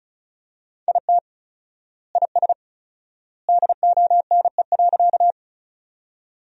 Morse code, heard as a keyed tone, about 34 wpm, 700 Hz. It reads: IT IS DONE.
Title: 4x Word-Spacing - Sentences from Top 500 Words - 35wpm